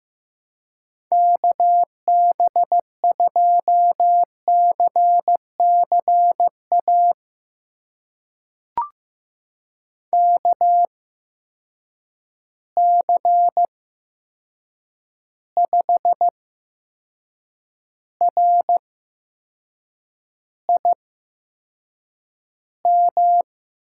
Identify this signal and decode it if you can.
Morse code, heard as a keyed tone, about 15 wpm, spaced as for 6 wpm, 700 Hz. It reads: KB2CCA KC5RIM
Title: USA Call Signs - 6 Characters - 8x Character Spacing - 15wpm